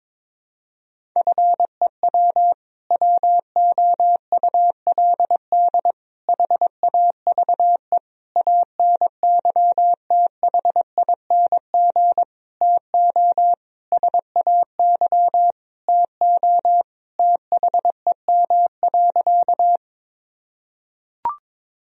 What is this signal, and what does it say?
Morse code, heard as a keyed tone, about 21 wpm, 700 Hz. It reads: FEW WOULD HAVE ANYTHING TO SAY TO THEM.
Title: Sentences from Top 1000 Words - Review Part 1 - 22wpm